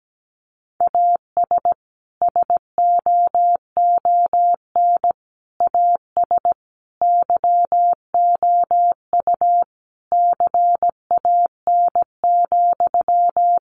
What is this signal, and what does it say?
Morse code, heard as a keyed tone, about 17 wpm, 700 Hz. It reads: AS SOON AS YOU CAN,
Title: Sentences from Top 1500 Words 17wpm